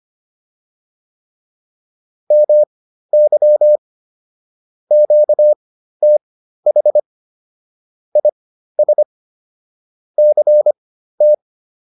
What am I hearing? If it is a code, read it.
Morse code, heard as a keyed tone, about 25 wpm, spaced as for 13 wpm, 600 Hz. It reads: MY QTH IS CT